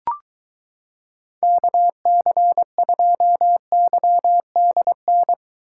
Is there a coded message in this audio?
Yes: Morse code from 1.4 to 5.3 s, heard as a keyed tone, about 23 wpm, 700 Hz. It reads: KC2YDN